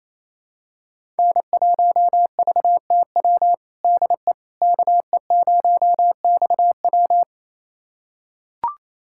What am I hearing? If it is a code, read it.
Morse code, heard as a keyed tone, about 27 wpm, 700 Hz. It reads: N1VTW DE KE0XW
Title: USA Callsign DE Callsign - 5 Characters - 28wpm